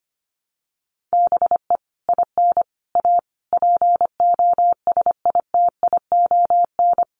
Morse code, heard as a keyed tone, about 25 wpm, 700 Hz. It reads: BE IN A POSITION
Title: Sets of 5 Words Encore2 25wpm